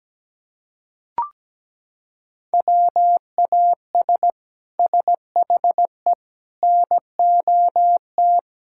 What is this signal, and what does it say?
Morse code, heard as a keyed tone, about 17 wpm, 700 Hz. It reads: WAS SHE NOT
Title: Sentences from Top 1000 Words - Review Part 5 - 17wpm